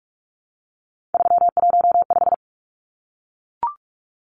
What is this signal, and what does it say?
Morse code, heard as a keyed tone, about 44 wpm, 700 Hz. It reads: 315
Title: Numbers - 3-Digit 45wpm